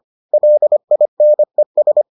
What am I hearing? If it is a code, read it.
Morse code, heard as a keyed tone, about 25 wpm, 600 Hz. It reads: LINES